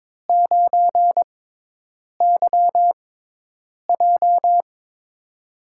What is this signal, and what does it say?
Morse code, heard as a keyed tone, about 22 wpm, spaced as for 8 wpm, 700 Hz. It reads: 9YJ
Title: USA Call Signs - 4 Characters - 6x Character Spacing - 22wpm